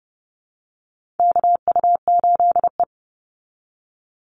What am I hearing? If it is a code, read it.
Morse code, heard as a keyed tone, about 30 wpm, 700 Hz. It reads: KU8E
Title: Call Signs Repeated 3x - 30wpm